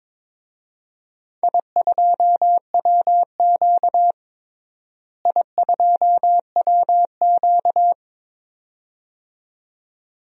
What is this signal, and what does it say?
Morse code, heard as a keyed tone, about 22 wpm, 700 Hz. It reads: I2WQ I2WQ
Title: DXCC-100 Call Signs - 4 Characters - Repeated 2x - 22wpm